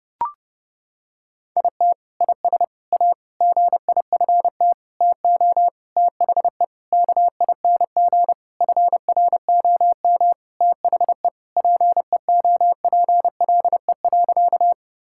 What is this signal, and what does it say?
Morse code, heard as a keyed tone, about 32 wpm, 700 Hz. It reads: IT IS A GIFT TO THE KING FROM THE PEOPLE.
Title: Sentences from Top 1000 Words - Review Part 6 - 30wpm